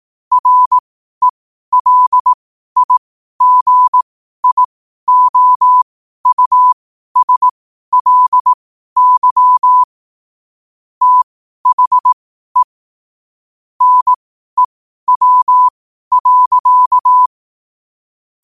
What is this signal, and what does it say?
Morse code, heard as a keyed tone, about 18 wpm, spaced as for 12 wpm, 1000 Hz. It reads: RELIGIOUSLY THE NEW.
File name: qotdfiles / 12wpm